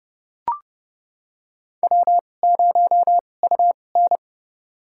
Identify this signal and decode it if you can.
Morse code, heard as a keyed tone, about 30 wpm, spaced as for 22 wpm, 700 Hz. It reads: W0UN
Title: USA Call Signs - 4 Characters - 2x Character Spacing - 30wpm